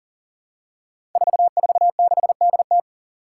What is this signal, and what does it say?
Morse code, heard as a keyed tone, about 40 wpm, 700 Hz. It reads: 446DT
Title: Five Challenging Letter-Number Combinations 40wpm